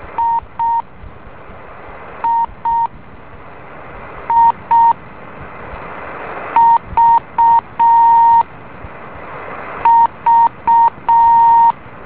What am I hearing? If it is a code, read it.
Morse code, heard as a keyed tone, about 6 wpm, 920 Hz. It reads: I I I V V
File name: beacon / JH8ZND